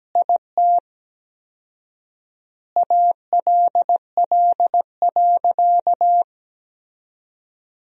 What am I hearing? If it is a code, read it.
Morse code, heard as a keyed tone, about 17 wpm, 700 Hz. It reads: IT ALL.
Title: 4x Word-Spacing - Sentences from Top 200 Words - 17wpm